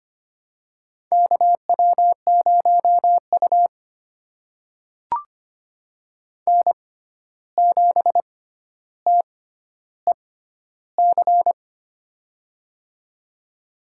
Morse code, heard as a keyed tone, about 25 wpm, spaced as for 12 wpm, 700 Hz. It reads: KW0U N7TEC
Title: USA Call Signs - 6x Character Spacing - 25wpm